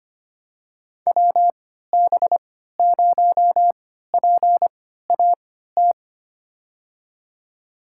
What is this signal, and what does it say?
Morse code, heard as a keyed tone, about 25 wpm, spaced as for 14 wpm, 700 Hz. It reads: WB0PAT
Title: USA Call Signs - 6 Characters - 3x Character Spacing - 25wpm